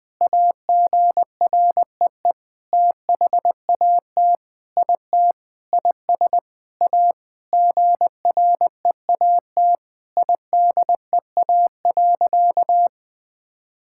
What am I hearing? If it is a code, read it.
Morse code, heard as a keyed tone, about 20 wpm, 700 Hz. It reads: AGREE THAT IT IS A GREAT IDEA.
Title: Sentences with 7-Word Phrases - Spoken Beginning - 20wpm